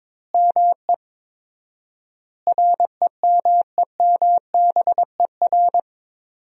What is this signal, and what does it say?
Morse code, heard as a keyed tone, about 22 wpm, 700 Hz. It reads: ME REMEMBER
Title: 4x Word-Spacing - Sentences from Top 400 Words - 22wpm